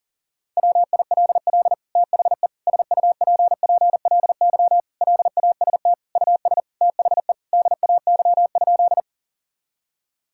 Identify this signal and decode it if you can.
Morse code, heard as a keyed tone, about 40 wpm, 700 Hz. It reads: WILL THE SUPPLY LAST US THE DAY?
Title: Sentences from Top 1000 Words - Review Part 5 - 40wpm